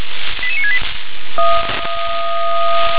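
Five-tone selective call (ZVEI), heard as quick runs of five tones, tones of about 70 ms. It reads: 88968
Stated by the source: ZVEI five-tone selective call received on SDR.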